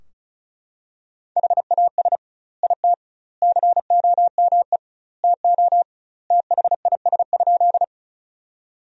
Morse code, heard as a keyed tone, about 36 wpm, 700 Hz. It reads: HAS IT COME TO THIS?
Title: 2x Word-Spacing - Sentences from Top 100 Words - 35wpm